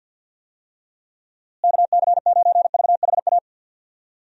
Morse code, heard as a keyed tone, about 50 wpm, 700 Hz. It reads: KC9VHA
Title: USA Call Signs - 6 Characters - Repeated 3x - 50wpm